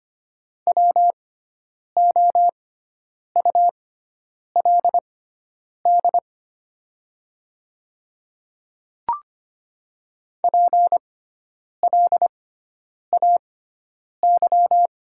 Morse code, heard as a keyed tone, about 25 wpm, spaced as for 9 wpm, 700 Hz. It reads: WOULD PLAY
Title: 6x Character Spacing - Top 500 Words - 25wpm